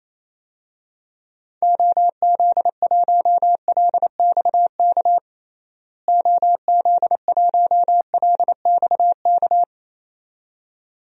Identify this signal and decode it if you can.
Morse code, heard as a keyed tone, about 27 wpm, 700 Hz. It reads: OZ1LXK OZ1LXK
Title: DXCC-100 Call Signs - 6 Characters - Repeated 2x - 28wpm